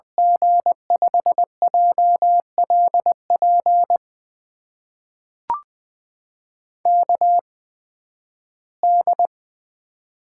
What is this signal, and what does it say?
Morse code, heard as a keyed tone, about 21 wpm, spaced as for 13 wpm, 700 Hz. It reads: G5JLP KD
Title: USA Call Signs - 6 Characters - 8x Character Spacing - 20wpm